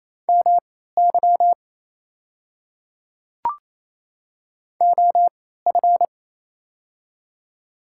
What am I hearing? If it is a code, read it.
Morse code, heard as a keyed tone, about 28 wpm, spaced as for 16 wpm, 700 Hz. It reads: MY OF
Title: 3x Character Spacing - Top 100 Words - 28wpm